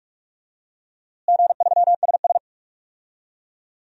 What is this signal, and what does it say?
Morse code, heard as a keyed tone, about 42 wpm, 700 Hz. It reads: G3SS
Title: DXCC-100 Call Signs - 4 Characters - Repeated 3x - 45wpm